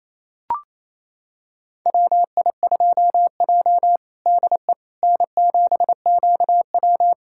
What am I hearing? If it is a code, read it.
Morse code, heard as a keyed tone, about 28 wpm, 700 Hz. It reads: WI2J DE N7QW
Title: USA Callsign DE Callsign - 4 Characters - 28wpm